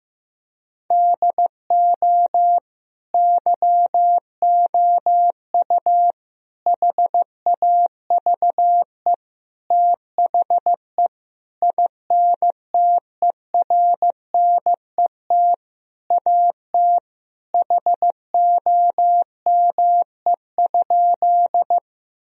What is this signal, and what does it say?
Morse code, heard as a keyed tone, about 15 wpm, 700 Hz. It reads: DO YOU HAVE THE INTERNET AT HOME?
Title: Sentences from Top 1000 Words - Review Part 4 - 15wpm